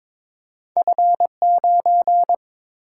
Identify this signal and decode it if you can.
Morse code, heard as a keyed tone, about 31 wpm, spaced as for 27 wpm, 700 Hz. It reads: F9
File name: Speed-Racing - Two Letter-Number Combinations - 22wpm